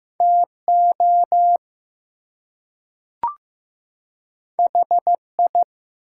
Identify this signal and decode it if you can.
Morse code, heard as a keyed tone, about 15 wpm, 700 Hz. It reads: TO HI
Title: Two Letter Words - 15wpm